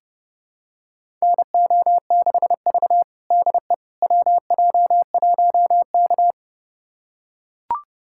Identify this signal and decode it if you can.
Morse code, heard as a keyed tone, about 30 wpm, 700 Hz. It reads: NO6V DE WJ1K
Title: USA Callsign DE Callsign - 4 Characters - 30wpm